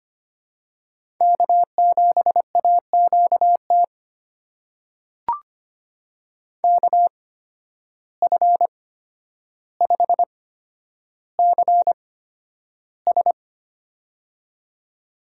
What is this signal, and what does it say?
Morse code, heard as a keyed tone, about 25 wpm, spaced as for 11 wpm, 700 Hz. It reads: K7AQT KF5CS
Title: USA Call Signs - 8x Character Spacing - 25wpm